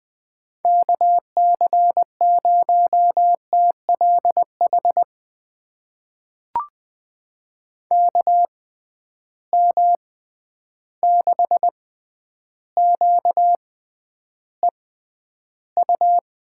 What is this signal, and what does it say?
Morse code, heard as a keyed tone, about 20 wpm, spaced as for 10 wpm, 700 Hz. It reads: KC0TLH KM6QEU
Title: USA Call Signs - 6 Characters - 6x Character Spacing - 20wpm